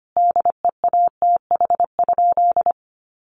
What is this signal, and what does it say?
Morse code, heard as a keyed tone, about 25 wpm, 700 Hz. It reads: DEATH?
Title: Sentences from Top 1100 Words 25wpm